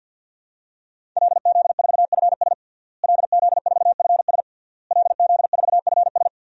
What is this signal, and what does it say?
Morse code, heard as a keyed tone, about 49 wpm, 700 Hz. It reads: LZ4FS LZ4FS LZ4FS
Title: DXCC-100 Call Signs - 5 Characters - Repeated 3x - 50wpm